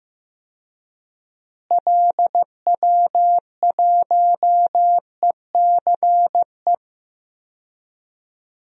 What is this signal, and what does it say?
Morse code, heard as a keyed tone, about 15 wpm, 700 Hz. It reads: LW1ECE
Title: DXCC-100 Call Signs - 6 Characters - Repeated 3x - 15wpm